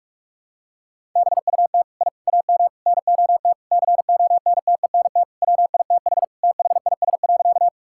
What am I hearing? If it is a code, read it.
Morse code, heard as a keyed tone, about 46 wpm, 700 Hz. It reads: BUT I AM NOT CONTENT WITH THIS.
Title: Sentences from Top 1000 Words - Review Part 6 - 45wpm